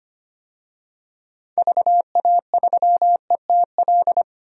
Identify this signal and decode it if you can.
Morse code, heard as a keyed tone, about 25 wpm, 700 Hz. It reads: VA3ETL
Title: DXCC-100 Call Signs - 6 Characters - 25wpm